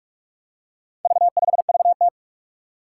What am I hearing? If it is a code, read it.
Morse code, heard as a keyed tone, about 44 wpm, 700 Hz. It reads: V5VT
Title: Four Challenging Letter-Number Combinations 45wpm